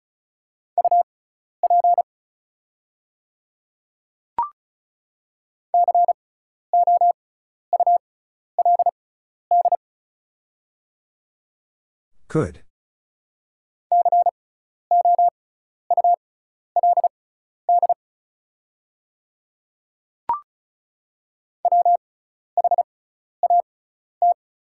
Morse code, heard as a keyed tone, about 36 wpm, spaced as for 12 wpm, 700 Hz. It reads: UP COULD COULD WHAT
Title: 6x Character Spacing - Top 100 Words - 35wpm